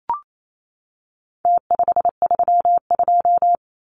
Morse code, heard as a keyed tone, about 28 wpm, 700 Hz. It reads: T532
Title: Numbers - Contest Style - 4-Digit 28wpm